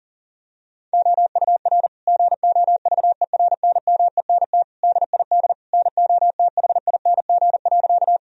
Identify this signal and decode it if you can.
Morse code, heard as a keyed tone, about 40 wpm, 700 Hz. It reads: OUR GOVERNMENT DID NOTHING.
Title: Sentences from Top 1000 Words - Review Part 3 - 40wpm